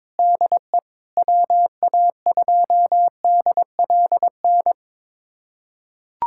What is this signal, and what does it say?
Morse code, heard as a keyed tone, about 22 wpm, 700 Hz. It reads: DE WA2DLN
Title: USA Callsign DE Callsign - 6 Characters - 22wpm